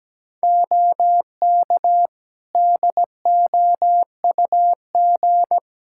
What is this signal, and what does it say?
Morse code, heard as a keyed tone, about 17 wpm, 700 Hz. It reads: OK DOUG
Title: Practice QSO - Exchange 2 - Info - 17wpm